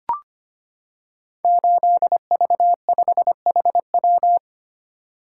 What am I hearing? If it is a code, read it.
Morse code, heard as a keyed tone, about 25 wpm, 700 Hz. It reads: 8V5HW
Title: Code Groups - 5 Letter:Numbers - 25wpm